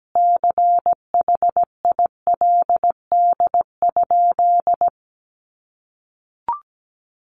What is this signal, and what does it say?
Morse code, heard as a keyed tone, about 17 wpm, 700 Hz. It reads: CHILD?
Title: Sentences from Top 800 Words 17wpm